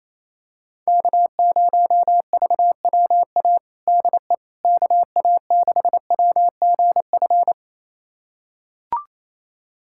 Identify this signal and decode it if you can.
Morse code, heard as a keyed tone, about 28 wpm, 700 Hz. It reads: K0VWA DE KA6WGF
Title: USA Callsign DE Callsign 28wpm